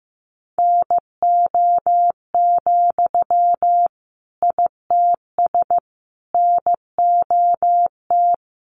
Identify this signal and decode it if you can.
Morse code, heard as a keyed tone, about 15 wpm, 700 Hz. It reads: NO, ITS NOT
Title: Sentences from Top 600 Words 15wpm